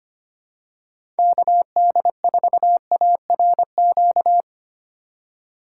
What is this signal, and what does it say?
Morse code, heard as a keyed tone, about 25 wpm, 700 Hz. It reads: KD4ARQ
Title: USA Call Signs - 6 Characters - 8x Character Spacing - 25wpm